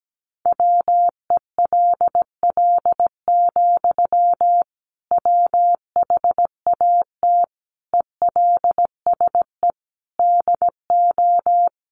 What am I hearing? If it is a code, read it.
Morse code, heard as a keyed tone, about 17 wpm, 700 Hz. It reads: WELL, WHAT ELSE DO
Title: Sentences from Top 800 Words 17wpm